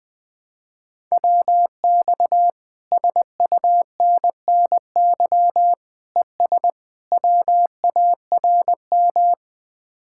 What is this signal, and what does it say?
Morse code, heard as a keyed tone, about 20 wpm, 700 Hz. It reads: WX SUNNY ES WARM
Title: Speed-Racing Common QSO Phrases - 20wpm